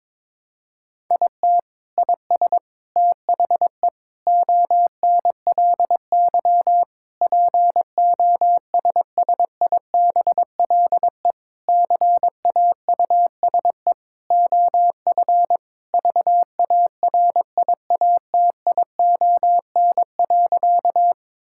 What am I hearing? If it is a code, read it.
Morse code, heard as a keyed tone, about 22 wpm, 700 Hz. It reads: IT IS THE ONLY POSSIBLE CAUSE OF VARIATION.
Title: Sentences from Top 1600 Words 22wpm